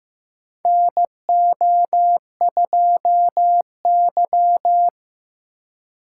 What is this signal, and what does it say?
Morse code, heard as a keyed tone, about 15 wpm, 700 Hz. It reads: NO2Y
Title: USA Call Signs - 2x Character Spacing - 15wpm